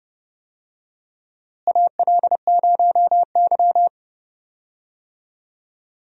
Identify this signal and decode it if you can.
Morse code, heard as a keyed tone, about 30 wpm, 700 Hz. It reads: AL0Y
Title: USA Call Signs - 4 Characters - Repeated 3x - 30wpm